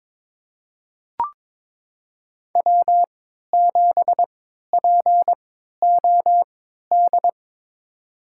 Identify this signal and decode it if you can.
Morse code, heard as a keyed tone, about 22 wpm, spaced as for 12 wpm, 700 Hz. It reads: W7POD
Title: USA Call Signs - 3x Character Spacing - 22wpm